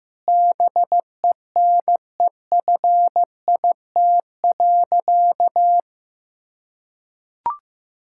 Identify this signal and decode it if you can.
Morse code, heard as a keyed tone, about 16 wpm, 700 Hz. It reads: BENEFIT.
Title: Sentences from Top 1000 Words - Review Part 6 - 15wpm